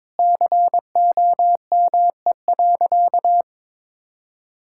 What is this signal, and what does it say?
Morse code, heard as a keyed tone, about 22 wpm, 700 Hz. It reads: COME.
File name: practice / 22wpm / Sentences with 2-Word Phrases - 22wpm